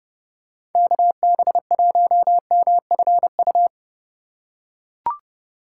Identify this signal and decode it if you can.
Morse code, heard as a keyed tone, about 31 wpm, 700 Hz. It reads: KB1MFU
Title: USA Call Signs - 6 Characters - 4x Character Spacing - 30wpm